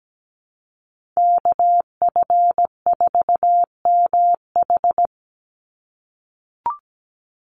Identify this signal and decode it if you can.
Morse code, heard as a keyed tone, about 17 wpm, 700 Hz. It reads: KF4MH
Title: NA SOTA Chaser Callsigns 17wpm